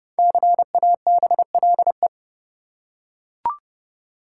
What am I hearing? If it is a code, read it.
Morse code, heard as a keyed tone, about 31 wpm, 700 Hz. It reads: CABLE
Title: Five Letter Words - 30wpm